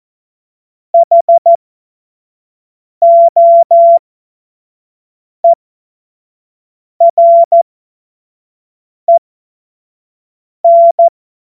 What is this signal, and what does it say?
Morse code, heard as a keyed tone, about 14 wpm, spaced as for 5 wpm, 680 Hz. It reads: HOEREN
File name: Morsekurs / Witze / Witz 08. August 2023 25 F70